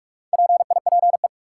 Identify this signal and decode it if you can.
Morse code, heard as a keyed tone, about 46 wpm, 700 Hz. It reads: PIPE
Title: Sentences from Top 1500 Words - New Word Drill - 45wpm